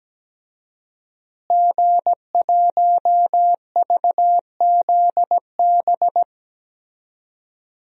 Morse code, heard as a keyed tone, about 17 wpm, 700 Hz. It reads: G1VZB
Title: DXCC-100 Call Signs - 5 Characters - Repeated 3x - 17wpm